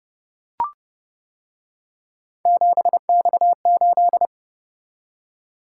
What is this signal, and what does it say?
Morse code, heard as a keyed tone, about 29 wpm, 700 Hz. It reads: 7X8